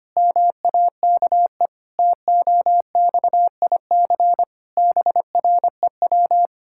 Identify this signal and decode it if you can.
Morse code, heard as a keyed tone, about 25 wpm, 700 Hz. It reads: MAKE TOXIC BREW
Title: Pangram Challenge v2 - 25wpm